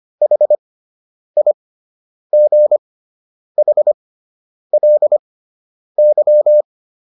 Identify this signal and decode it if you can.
Morse code, heard as a keyed tone, about 25 wpm, spaced as for 9 wpm, 600 Hz. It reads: HIGHLY